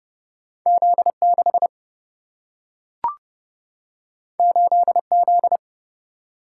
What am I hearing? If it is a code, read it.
Morse code, heard as a keyed tone, about 30 wpm, 700 Hz. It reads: Z6 8Z
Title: Single-Letter Single-Number 30wpm